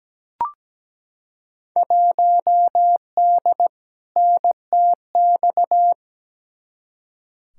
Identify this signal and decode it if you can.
Morse code, heard as a keyed tone, about 17 wpm, 700 Hz. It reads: 1D NTX
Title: Field Day Exchanges - 17wpm